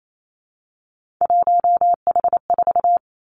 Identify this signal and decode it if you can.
Morse code, heard as a keyed tone, about 28 wpm, 700 Hz. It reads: 1H4
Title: Three Challenging Letter-Number Combinations 28wpm